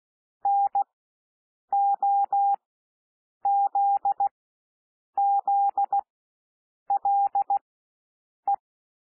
Morse code, heard as a keyed tone, about 16 wpm, spaced as for 7 wpm, 800 Hz. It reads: NOZZLE
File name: cwmp3 / W1 SOUND WORDS 16 wpm